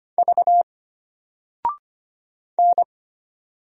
Morse code, heard as a keyed tone, about 25 wpm, 700 Hz. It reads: V N
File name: Single-Letter-Number-25wpm